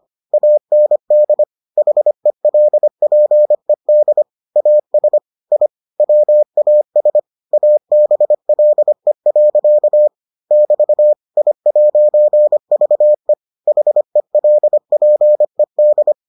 Morse code, heard as a keyed tone, about 25 wpm, 600 Hz. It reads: AND HELPED AS I WAS ABLE. <BT> I'VE HELPED